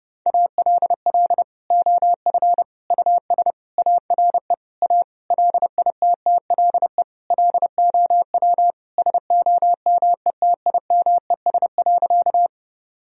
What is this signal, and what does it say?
Morse code, heard as a keyed tone, about 30 wpm, 700 Hz. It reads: ALL OF US ARE A LITTLE LOW SOMETIMES.
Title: Sentences from Top 1000 Words - Review Part 3 - 30wpm